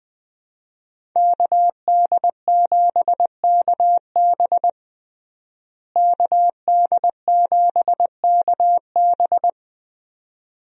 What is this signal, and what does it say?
Morse code, heard as a keyed tone, about 20 wpm, 700 Hz. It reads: KD7KB KD7KB
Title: USA Call Signs - 5 Characters - Repeated 2x - 20wpm